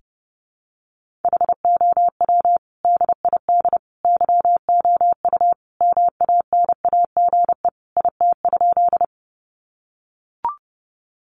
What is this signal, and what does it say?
Morse code, heard as a keyed tone, about 30 wpm, 700 Hz. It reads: HOW DID YOU MANAGE IT?
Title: Sentences from Top 1400 Words 30wpm